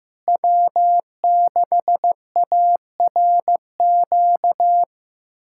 Morse code, heard as a keyed tone, about 15 wpm, 700 Hz. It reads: W6ARQ
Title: USA Callsign DE Callsign - 5 Characters - 15wpm